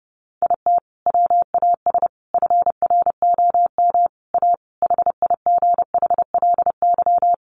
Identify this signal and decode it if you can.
Morse code, heard as a keyed tone, about 30 wpm, 700 Hz. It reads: IT WAS FROM A HIGHLY
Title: Sentences from Top 1100 Words 30wpm